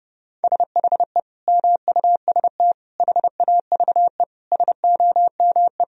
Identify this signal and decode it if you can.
Morse code, heard as a keyed tone, about 32 wpm, 700 Hz. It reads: SHE MUST HAVE SOME
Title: Sentences from Top 1000 Words - Review Part 1 - 30wpm